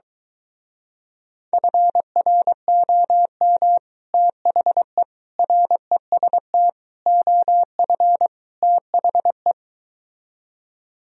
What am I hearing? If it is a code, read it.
Morse code, heard as a keyed tone, about 22 wpm, 700 Hz. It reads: FROM THE REST OF THE